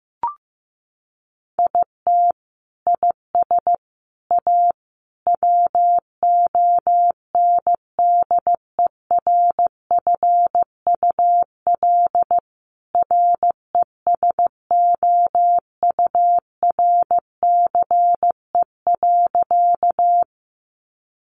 Morse code, heard as a keyed tone, about 15 wpm, 700 Hz. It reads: IT IS A WONDERFUL RESOURCE.
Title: Sentences from Top 1300 Words 15wpm